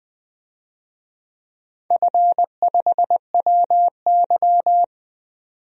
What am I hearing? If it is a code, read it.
Morse code, heard as a keyed tone, about 20 wpm, 700 Hz. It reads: F5WY
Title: DXCC-100 Call Signs - 4 Characters - 20wpm